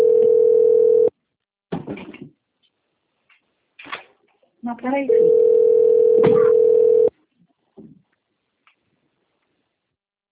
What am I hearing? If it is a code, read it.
Ringback tone (North America), 440 + 480 Hz, with speech.
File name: out- - - - - .1904